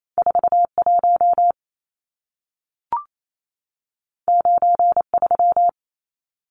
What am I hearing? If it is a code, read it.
Morse code, heard as a keyed tone, about 28 wpm, 700 Hz. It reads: 41 93
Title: Numbers - 2-Digit 28wpm